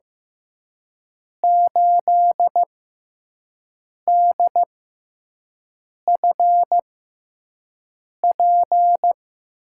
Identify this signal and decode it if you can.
Morse code, heard as a keyed tone, about 15 wpm, spaced as for 5 wpm, 700 Hz. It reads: 8DFP
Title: USA Call Signs - 6x Character Spacing - 15wpm